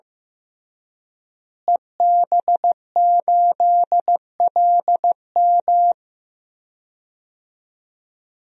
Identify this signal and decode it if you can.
Morse code, heard as a keyed tone, about 15 wpm, 700 Hz. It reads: EB8LM
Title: DXCC-100 Call Signs - 5 Characters - Repeated 3x - 15wpm